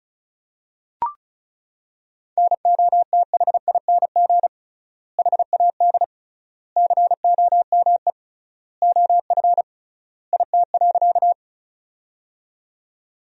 Morse code, heard as a keyed tone, about 34 wpm, 700 Hz. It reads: NOTHING HAD COME OF IT.
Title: 3x Word-Spacing - Sentences from Top 500 Words - 35wpm